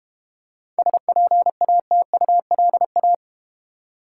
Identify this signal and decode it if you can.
Morse code, heard as a keyed tone, about 32 wpm, 700 Hz. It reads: SPATULA